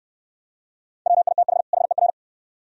Morse code, heard as a keyed tone, about 63 wpm, 700 Hz. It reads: WITH HER